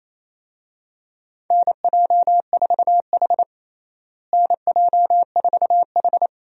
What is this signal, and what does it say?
Morse code, heard as a keyed tone, about 28 wpm, 700 Hz. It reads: NJ4H NJ4H
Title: USA Call Signs - 4 Characters - Repeated 2x - 28wpm